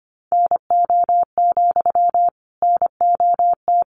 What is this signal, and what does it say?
Morse code, heard as a keyed tone, about 25 wpm, 700 Hz. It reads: NO, NOT
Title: Sentences from Top 1000 Words 25wpm